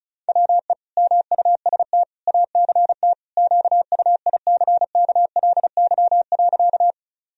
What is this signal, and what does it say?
Morse code, heard as a keyed tone, about 36 wpm, 700 Hz. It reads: WE MUST ACT QUICKLY.
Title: Sentences from Top 1000 Words - Review Part 1 - 35wpm